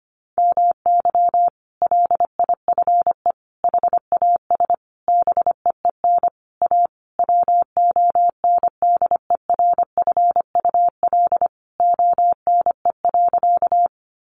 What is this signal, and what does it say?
Morse code, heard as a keyed tone, about 25 wpm, 700 Hz. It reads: MY LIFE HAS BEEN A WONDERFUL ONE.
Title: Sentences from Top 1100 Words 25wpm